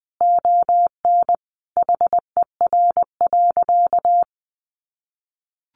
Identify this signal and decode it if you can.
Morse code, heard as a keyed tone, about 20 wpm, 700 Hz. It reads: ON HER.
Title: Sentences from Top 1300 Words 20wpm